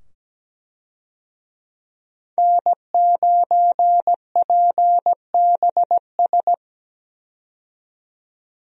Morse code, heard as a keyed tone, about 17 wpm, 700 Hz. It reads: N9PBS
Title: USA Call Signs - 5 Characters - 17wpm